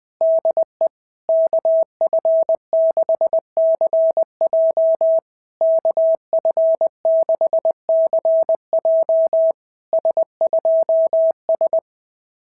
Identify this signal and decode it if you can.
Morse code, heard as a keyed tone, about 20 wpm, 650 Hz. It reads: DE KF6CJ KF6CJ S2S
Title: SOTA QSO Practice 20wpm